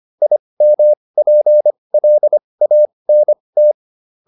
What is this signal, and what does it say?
Morse code, heard as a keyed tone, about 25 wpm, spaced as for 20 wpm, 600 Hz. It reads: IMPLANT